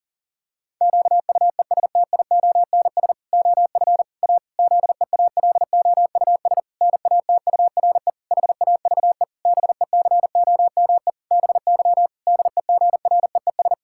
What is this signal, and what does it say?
Morse code, heard as a keyed tone, about 40 wpm, 700 Hz. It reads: QUESTIONS OF A ZEALOUS NATURE HAVE BECOME BY DEGREES
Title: Pangram Challenge v2 - 40wpm